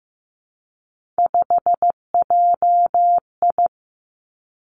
Morse code, heard as a keyed tone, about 15 wpm, 700 Hz. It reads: 5JI
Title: Three Challenging Letter-Number Combinations 15wpm